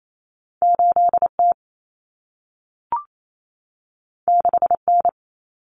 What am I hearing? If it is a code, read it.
Morse code, heard as a keyed tone, about 28 wpm, 700 Hz. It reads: 8T 6N
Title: Common Cut Numbers - 2-Digit 28wpm